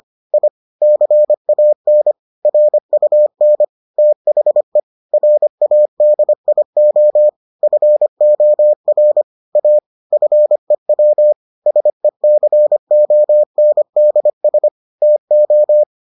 Morse code, heard as a keyed tone, about 25 wpm, 600 Hz. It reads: I CAN RUN THE RADIO FOR A FEW SECONDS TO